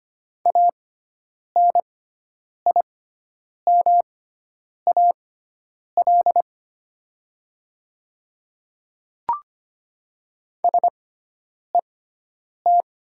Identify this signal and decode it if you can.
Morse code, heard as a keyed tone, about 25 wpm, spaced as for 9 wpm, 700 Hz. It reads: ANIMAL SET
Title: 6x Character Spacing - Top 200 Words - 25wpm